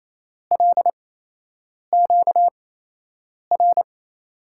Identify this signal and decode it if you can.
Morse code, heard as a keyed tone, about 28 wpm, spaced as for 8 wpm, 700 Hz. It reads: LQR
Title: USA Call Signs - 8x Character Spacing - 28wpm